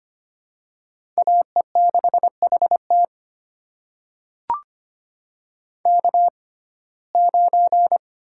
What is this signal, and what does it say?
Morse code, heard as a keyed tone, about 25 wpm, spaced as for 18 wpm, 700 Hz. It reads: AE6HT K9
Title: USA Call Signs - 6x Character Spacing - 25wpm